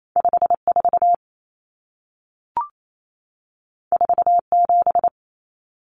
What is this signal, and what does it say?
Morse code, heard as a keyed tone, about 28 wpm, 700 Hz. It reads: 54 47
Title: Numbers - 2-Digit 28wpm